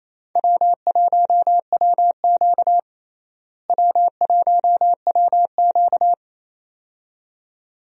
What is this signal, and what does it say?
Morse code, heard as a keyed tone, about 28 wpm, 700 Hz. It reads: W1WQ W1WQ
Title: USA Call Signs - 4 Characters - Repeated 2x - 28wpm